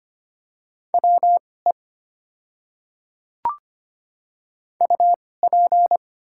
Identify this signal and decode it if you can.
Morse code, heard as a keyed tone, about 25 wpm, spaced as for 18 wpm, 690 Hz. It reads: WE UP
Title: 2x Character Spacing - Top 100 Words - 25wpm